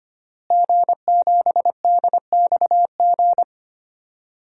Morse code, heard as a keyed tone, about 25 wpm, 700 Hz. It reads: G7DXG
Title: DXCC-100 Call Signs - 5 Characters - Repeated 3x - 25wpm